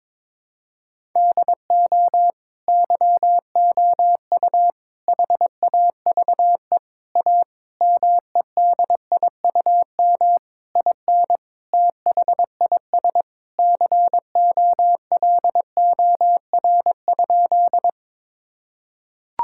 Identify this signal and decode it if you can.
Morse code, heard as a keyed tone, about 22 wpm, 700 Hz. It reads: DO YOU HAVE A MEDIUM IN THIS COLOR?
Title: Sentences from Top 1000 Words - Review Part 6 - 22wpm